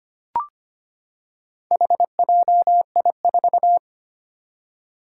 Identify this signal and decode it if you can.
Morse code, heard as a keyed tone, about 25 wpm, 700 Hz. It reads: HJI4
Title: Four Challenging Letter-Number Combinations 25wpm